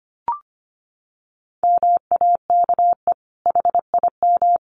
Morse code, heard as a keyed tone, about 25 wpm, 700 Hz. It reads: MAKE HIM
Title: Sentences from Top 900 Words 25wpm